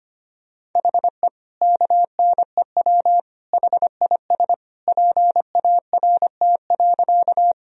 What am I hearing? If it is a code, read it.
Morse code, heard as a keyed tone, about 25 wpm, 700 Hz. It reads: HE KNEW HIS PART.
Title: Sentences from Top 1000 Words - Review Part 1 - 25wpm